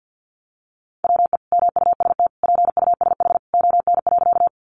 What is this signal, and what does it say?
Morse code, heard as a keyed tone, about 50 wpm, 700 Hz. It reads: WE MUST PUSH ON.
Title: Sentences from Top 1300 Words 50wpm